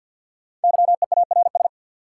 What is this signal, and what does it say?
Morse code, heard as a keyed tone, about 52 wpm, 700 Hz. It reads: YEARS
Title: Sentences from Top 200 Words - New Word Drill - 50wpm